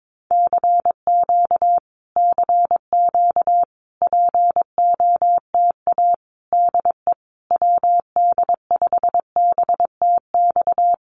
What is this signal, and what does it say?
Morse code, heard as a keyed tone, about 22 wpm, 700 Hz. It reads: CQ CQ POTA DE WD5BTX
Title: Common QSO Phrases Encore 22wpm